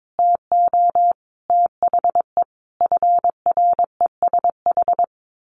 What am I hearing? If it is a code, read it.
Morse code, heard as a keyed tone, about 22 wpm, 700 Hz. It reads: TO THE FRESH
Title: A Sound Of Thunder 22wpm